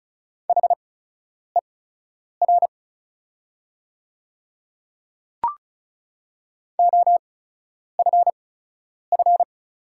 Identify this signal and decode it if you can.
Morse code, heard as a keyed tone, about 36 wpm, spaced as for 10 wpm, 700 Hz. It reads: HER OFF
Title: 8x Character Spacing - Top 500 Words - 35wpm